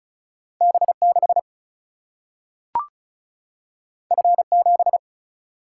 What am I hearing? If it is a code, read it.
Morse code, heard as a keyed tone, about 36 wpm, 700 Hz. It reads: B 6 F 7
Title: Single-Letter Single-Number 35wpm